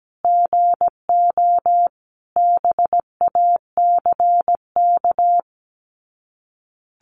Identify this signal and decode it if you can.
Morse code, heard as a keyed tone, about 17 wpm, 700 Hz. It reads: GO BACK
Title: Sets of 5 Words Encore 17wpm